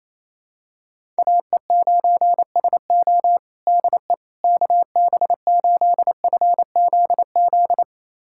Morse code, heard as a keyed tone, about 28 wpm, 700 Hz. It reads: AE9SO DE KB8FZZ
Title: USA Callsign DE Callsign 28wpm